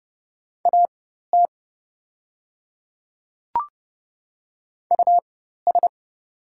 Morse code, heard as a keyed tone, about 30 wpm, spaced as for 14 wpm, 700 Hz. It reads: AT US
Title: 4x Character Spacing - Top 300 Words - 30wpm